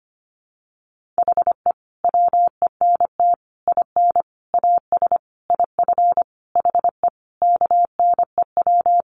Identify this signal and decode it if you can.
Morse code, heard as a keyed tone, about 25 wpm, 700 Hz. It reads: HE WENT IN AS IF HE KNEW
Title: Sentences from Top 1400 Words 25wpm